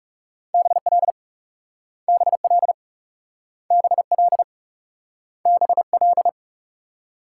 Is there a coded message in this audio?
Yes: Morse code from 0.5 to 6.3 s, heard as a keyed tone, about 40 wpm, 700 Hz. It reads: BL BL BL BL